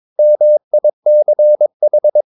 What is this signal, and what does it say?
Morse code, heard as a keyed tone, about 22 wpm, 600 Hz. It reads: MICH